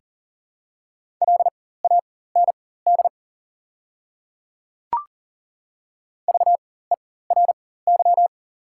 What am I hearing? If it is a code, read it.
Morse code, heard as a keyed tone, about 40 wpm, spaced as for 19 wpm, 700 Hz. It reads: LAND VERY
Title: 4x Character Spacing - Top 200 Words - 40wpm